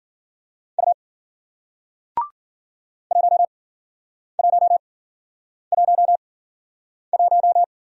Morse code, heard as a keyed tone, about 54 wpm, spaced as for 9 wpm, 700 Hz. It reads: U 1111